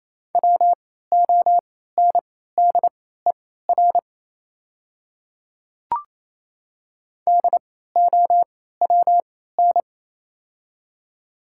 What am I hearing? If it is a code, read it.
Morse code, heard as a keyed tone, about 28 wpm, spaced as for 16 wpm, 700 Hz. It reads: WONDER DOWN
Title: 3x Character Spacing - Top 500 Words - 28wpm